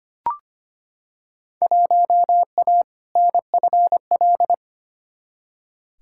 Morse code, heard as a keyed tone, about 25 wpm, 700 Hz. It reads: 1A NFL
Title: Field Day Exchanges - 25wpm